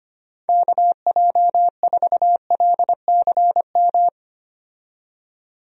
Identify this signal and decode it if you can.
Morse code, heard as a keyed tone, about 25 wpm, 700 Hz. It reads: KJ4LCM
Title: USA Call Signs - 6 Characters - 25wpm